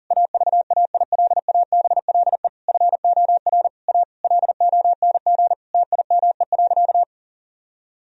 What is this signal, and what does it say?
Morse code, heard as a keyed tone, about 37 wpm, 700 Hz. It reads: AVAILABLE FOR A LONG TIME.
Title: Sentences from Top 1000 Words - Review Part 4 - 40wpm